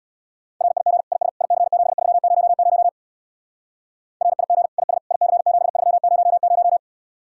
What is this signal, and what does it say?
Morse code, heard as a keyed tone, about 64 wpm, 700 Hz. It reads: RIG IS IC7300 RIG IS IC7300